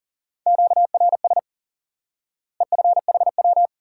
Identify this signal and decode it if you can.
Morse code, heard as a keyed tone, about 37 wpm, 700 Hz. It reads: QRS EFHW
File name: IQE Course 3 - Lesson 89 - YAGI - Two Elements - 40wpm